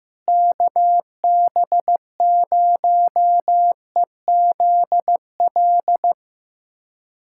Morse code, heard as a keyed tone, about 15 wpm, 700 Hz. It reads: KB0EZL
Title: USA Callsign DE Callsign - 6 Characters - 15wpm